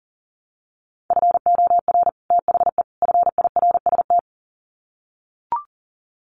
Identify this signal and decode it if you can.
Morse code, heard as a keyed tone, about 37 wpm, 700 Hz. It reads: FOR THE FIRST
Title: Sets of 3 Words 40wpm